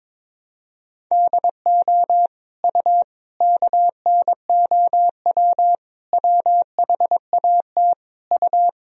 Morse code, heard as a keyed tone, about 21 wpm, 700 Hz. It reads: DO U KNOW WHAT U
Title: Sentences from Top 100 Words and Abbreviations 22wpm